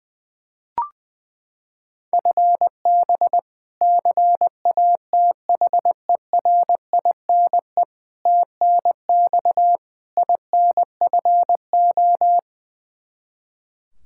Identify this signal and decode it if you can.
Morse code, heard as a keyed tone, about 20 wpm, 700 Hz. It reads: FB CATHERINE TNX INFO
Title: K1USN SST QSO Practice 20wpm